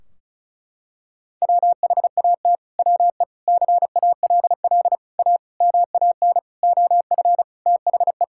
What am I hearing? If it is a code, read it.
Morse code, heard as a keyed tone, about 36 wpm, 700 Hz. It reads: WHAT WE CALL A MAN OF THE
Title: Sentences from Top 1600 Words 35wpm